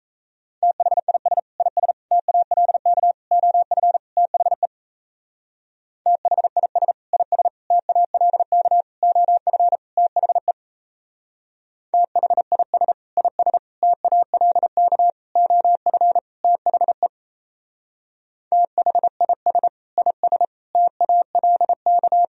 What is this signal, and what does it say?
Morse code, heard as a keyed tone, about 35 wpm, 700 Hz. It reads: THIS IS TALK OF THE THIS IS TALK OF THE THIS IS TALK OF THE THIS IS TALK
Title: Speed-Racing - Sets of 5 Words 28wpm